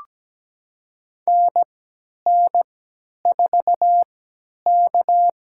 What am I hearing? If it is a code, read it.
Morse code, heard as a keyed tone, about 17 wpm, spaced as for 10 wpm, 700 Hz. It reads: NN4K
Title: USA Call Signs - 4 Characters - 3x Character Spacing - 17wpm